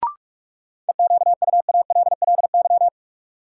Morse code, heard as a keyed tone, about 46 wpm, 700 Hz. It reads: EQUALLY
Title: Sentences from Top 1400 Words - New Word Drill - 45wpm